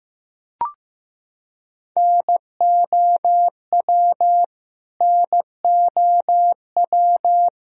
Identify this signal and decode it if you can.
Morse code, heard as a keyed tone, about 15 wpm, 700 Hz. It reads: NOW NOW
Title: Top 100 Words Repeated 15wpm